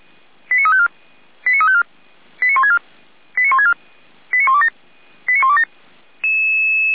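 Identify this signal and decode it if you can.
Five-tone selective call (ZVEI), heard as quick runs of five tones, tones of about 70 ms. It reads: 89354 89354 89164 89164 89127 89127
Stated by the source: ZVEI five-tone selective call received via SDR.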